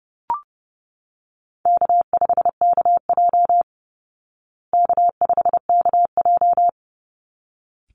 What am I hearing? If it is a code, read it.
Morse code, heard as a keyed tone, about 30 wpm, 700 Hz. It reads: K5KJ K5KJ
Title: Easy Call Signs Repeated 2x - 30wpm